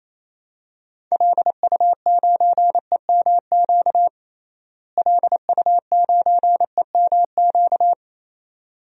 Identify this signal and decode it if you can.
Morse code, heard as a keyed tone, about 28 wpm, 700 Hz. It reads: LU9EMQ LU9EMQ
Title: DXCC-100 Call Signs - 6 Characters - Repeated 2x - 28wpm